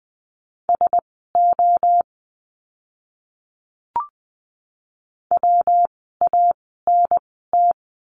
Morse code, heard as a keyed tone, about 20 wpm, spaced as for 14 wpm, 700 Hz. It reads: SO WANT
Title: 230 Lesson 37f - Letter x - Words 20wpm